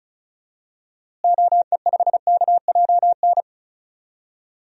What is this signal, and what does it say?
Morse code, heard as a keyed tone, about 34 wpm, 700 Hz. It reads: OE5KJN
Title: DXCC-100 Call Signs - 6 Characters - Repeated 3x - 35wpm